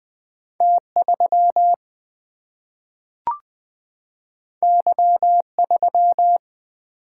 Morse code, heard as a keyed tone, about 20 wpm, 700 Hz. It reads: T 3 Y 3
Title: Single-Letter Single-Number 20wpm